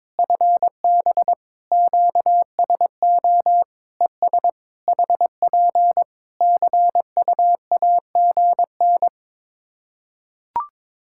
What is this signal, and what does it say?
Morse code, heard as a keyed tone, about 22 wpm, 700 Hz. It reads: FB QSO ES HP CUAGN
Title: K1USN SST QSO Practice 22wpm